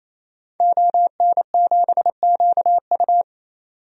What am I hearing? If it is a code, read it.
Morse code, heard as a keyed tone, about 27 wpm, 700 Hz. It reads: ON7QU
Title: DXCC-100 Call Signs - 5 Characters - 28wpm